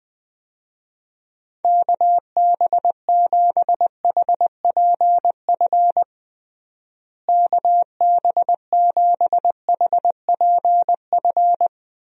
Morse code, heard as a keyed tone, about 20 wpm, 700 Hz. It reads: KB7HPF KB7HPF
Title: USA Call Signs - 6 Characters - Repeated 2x - 20wpm